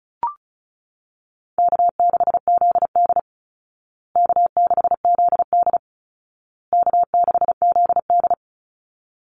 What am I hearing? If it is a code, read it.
Morse code, heard as a keyed tone, about 34 wpm, 700 Hz. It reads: K6ZD K6ZD K6ZD
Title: Easy Call Signs Repeated 3x - 35wpm